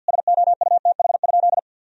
Morse code, heard as a keyed tone, about 51 wpm, 700 Hz. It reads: SOUTH?
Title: Sentences from Top 1000 Words - Review Part 2 - 50wpm